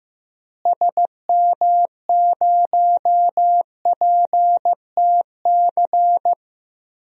Morse code, heard as a keyed tone, about 15 wpm, 700 Hz. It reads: SM0PTC
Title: DXCC-100 Call Signs - 6 Characters - Repeated 3x - 15wpm